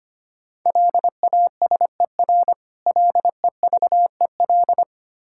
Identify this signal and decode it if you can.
Morse code, heard as a keyed tone, about 25 wpm, 700 Hz. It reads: LASER LEVEL
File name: practice / 25wpm / Difficult Words - Speed-Racing with Hints - 25wpm